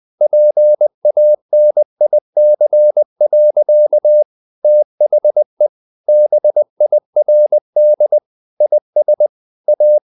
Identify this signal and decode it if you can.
Morse code, heard as a keyed tone, about 20 wpm, 600 Hz. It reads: PANIC. THE BIRD IS A
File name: the old man and the sea / the old man and the sea part 1 20wpm